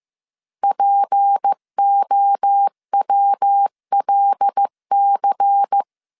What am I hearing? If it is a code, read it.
Morse code, heard as a keyed tone, about 15 wpm, 800 Hz. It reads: POWLC
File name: CW / DET / 1 6 1500C S C 14wpm 2951